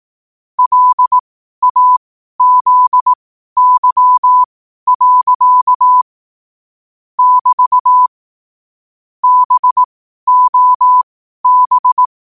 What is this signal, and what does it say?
Morse code, heard as a keyed tone, about 18 wpm, spaced as for 12 wpm, 1000 Hz. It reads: LAZY. <BT> BOB
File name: qotdfiles / 12wpm